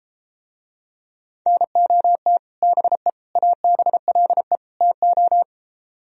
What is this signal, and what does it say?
Morse code, heard as a keyed tone, about 31 wpm, 700 Hz. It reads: NOT BE ABLE TO